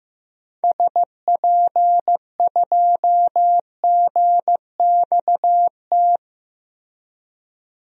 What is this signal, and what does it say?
Morse code, heard as a keyed tone, about 15 wpm, 700 Hz. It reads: SP2GXT
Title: DXCC-100 Call Signs - 6 Characters - Repeated 2x - 15wpm